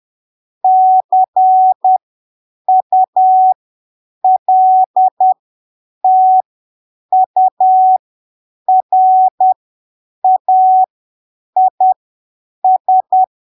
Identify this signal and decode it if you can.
Morse code, heard as a keyed tone, about 10 wpm, spaced as for 7 wpm, 750 Hz. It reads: CULTURAIS
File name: arquivos / texto / texto 30 10ppm